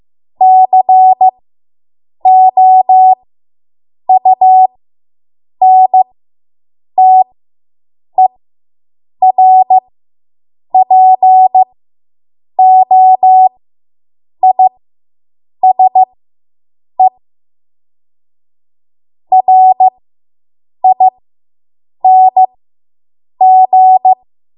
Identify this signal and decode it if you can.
Morse code, heard as a keyed tone, about 15 wpm, spaced as for 7 wpm, 750 Hz. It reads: COUNTERPOISE RING